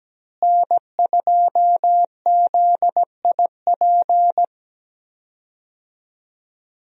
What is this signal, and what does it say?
Morse code, heard as a keyed tone, about 17 wpm, 700 Hz. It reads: N2ZIP
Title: USA Call Signs - 5 Characters - Repeated 3x - 17wpm